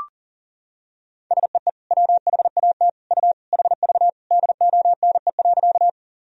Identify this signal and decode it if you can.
Morse code, heard as a keyed tone, about 40 wpm, 700 Hz. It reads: SEE WHAT U HV DONE.
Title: Sentences from Top 500 Words and Abbreviations 40wpm